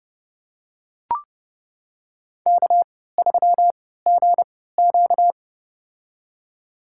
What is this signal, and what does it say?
Morse code, heard as a keyed tone, about 30 wpm, spaced as for 17 wpm, 700 Hz. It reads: K3GQ
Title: USA Call Signs - 4 Characters - 3x Character Spacing - 30wpm